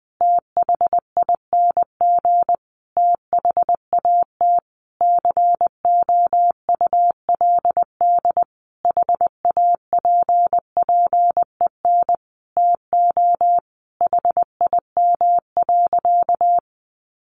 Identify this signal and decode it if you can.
Morse code, heard as a keyed tone, about 20 wpm, 700 Hz. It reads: THING THAT COULD HAPPEN TO HIM.
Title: Sentences from Top 400 Words 20wpm